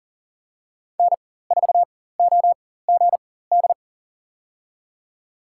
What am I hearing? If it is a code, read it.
Morse code, heard as a keyed tone, about 40 wpm, spaced as for 19 wpm, 700 Hz. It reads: N4OGD
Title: USA Call Signs - 5 Characters - 4x Character Spacing - 40wpm